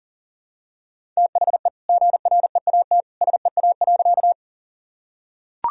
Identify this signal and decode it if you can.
Morse code, heard as a keyed tone, about 40 wpm, 700 Hz. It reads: THE GREAT SEA.
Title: Sentences from Top 1000 Words - Review Part 1 - 40wpm